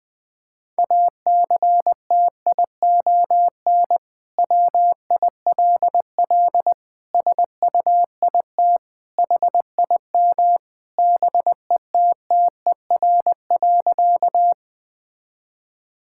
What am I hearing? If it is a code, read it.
Morse code, heard as a keyed tone, about 20 wpm, 700 Hz. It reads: ACTION WILL SUIT HIM BETTER.
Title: Sentences from Top 1600 Words 20wpm